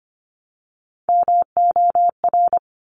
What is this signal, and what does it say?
Morse code, heard as a keyed tone, about 25 wpm, 700 Hz. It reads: MOR
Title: Top Three Letter Combinations in Words 25wpm